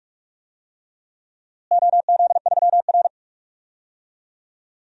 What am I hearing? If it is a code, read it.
Morse code, heard as a keyed tone, about 46 wpm, 700 Hz. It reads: OZ3R
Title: DXCC-100 Call Signs - 4 Characters - Repeated 2x - 45wpm